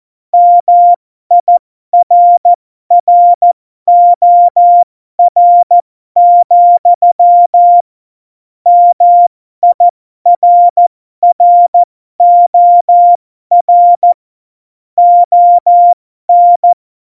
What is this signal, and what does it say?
Morse code, heard as a keyed tone, about 14 wpm, spaced as for 12 wpm, 700 Hz. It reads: MIRROR, MIRROR ON
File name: Morsekurs / Witze